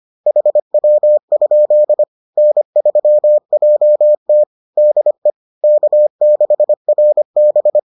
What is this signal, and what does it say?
Morse code, heard as a keyed tone, about 25 wpm, 600 Hz. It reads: HW? N3JT DE K6RB